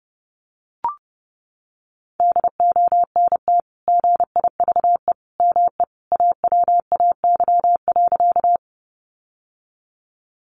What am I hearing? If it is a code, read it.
Morse code, heard as a keyed tone, about 30 wpm, 700 Hz. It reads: DONT GIVE ME AWAY.
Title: Sentences from Top 600 Words 30wpm